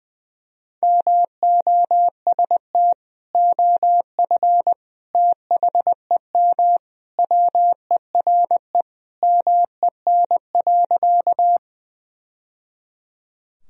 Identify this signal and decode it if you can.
Morse code, heard as a keyed tone, about 21 wpm, 700 Hz. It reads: MOST OF THEM WERE MEN.
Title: Sentences from Top 1000 Words - Review Part 1 - 20wpm